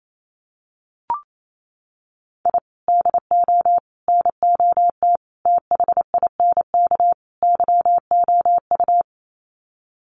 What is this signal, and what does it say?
Morse code, heard as a keyed tone, about 28 wpm, 700 Hz. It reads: I DO NOT THINK YOU
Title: Sets of 5 Words 28wpm